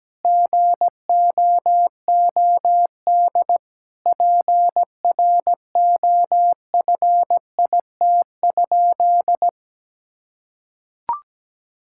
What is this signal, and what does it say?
Morse code, heard as a keyed tone, about 17 wpm, 700 Hz. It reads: GOOD PROFIT?
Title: Sentences from Top 1000 Words - Review Part 5 - 17wpm